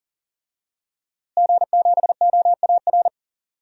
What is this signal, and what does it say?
Morse code, heard as a keyed tone, about 40 wpm, 700 Hz. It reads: G7OAR
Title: DXCC-100 Call Signs - 5 Characters - Repeated 3x - 40wpm